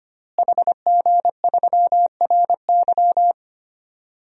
Morse code, heard as a keyed tone, about 25 wpm, 700 Hz. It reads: HG3RY
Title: DXCC-100 Call Signs - 5 Characters - Repeated 3x - 25wpm